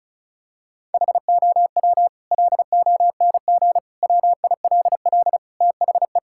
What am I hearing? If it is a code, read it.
Morse code, heard as a keyed tone, about 34 wpm, 700 Hz. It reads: HOW LONG WILL THE
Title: Sentences from Top 1000 Words - Review Part 5 - 35wpm